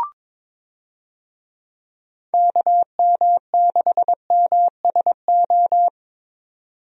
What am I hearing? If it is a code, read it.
Morse code, heard as a keyed tone, about 22 wpm, 700 Hz. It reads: KM6MSO
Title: USA Call Signs - 6 Characters - 22wpm